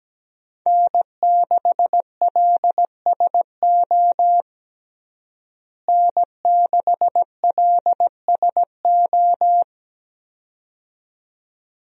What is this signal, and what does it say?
Morse code, heard as a keyed tone, about 17 wpm, 700 Hz. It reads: N6LSO N6LSO
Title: USA Call Signs - 5 Characters - Repeated 2x - 17wpm